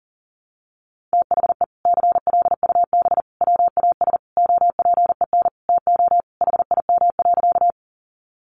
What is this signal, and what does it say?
Morse code, heard as a keyed tone, about 40 wpm, 700 Hz. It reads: THE CLUB WAS OPEN TO HIM.
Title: Sentences from Top 1200 Words 40wpm